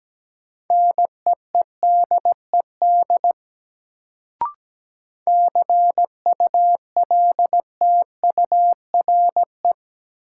Morse code, heard as a keyed tone, about 17 wpm, 700 Hz. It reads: NEEDED CULTURE
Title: Sentences from Top 1000 Words - Review Part 6 - New Word Drill - 17wpm